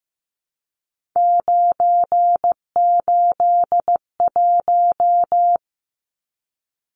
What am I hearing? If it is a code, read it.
Morse code, heard as a keyed tone, about 15 wpm, 700 Hz. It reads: 981
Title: Numbers - 3-Digit 15wpm